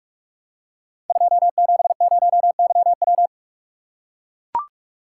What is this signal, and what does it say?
Morse code, heard as a keyed tone, about 44 wpm, 700 Hz. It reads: 270YW
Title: Code Groups - 5 Letter:Numbers - 45wpm